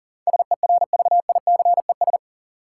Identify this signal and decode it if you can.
Morse code, heard as a keyed tone, about 40 wpm, 700 Hz. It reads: SERVICES
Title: Sets of 5 Words - New Word Drill - 40wpm